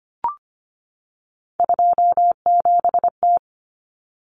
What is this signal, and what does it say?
Morse code, heard as a keyed tone, about 25 wpm, 700 Hz. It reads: 27T
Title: Common Cut Numbers - 3-Digit 25wpm